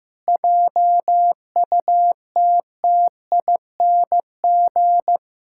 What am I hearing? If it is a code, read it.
Morse code, heard as a keyed tone, about 15 wpm, 700 Hz. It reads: JUTTING
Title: Pangram Challenge v2 - 15wpm